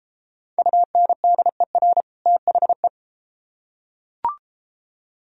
Morse code, heard as a keyed tone, about 32 wpm, 700 Hz. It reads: UNDER THE